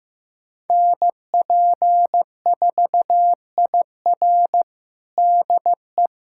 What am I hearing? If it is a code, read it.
Morse code, heard as a keyed tone, about 16 wpm, 700 Hz. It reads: NP4IR DE
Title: USA Callsign DE Callsign - 5 Characters - 15wpm